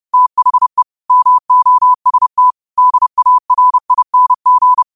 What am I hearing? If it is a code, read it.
Morse code, heard as a keyed tone, about 30 wpm, 1000 Hz. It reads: THE MOST DARING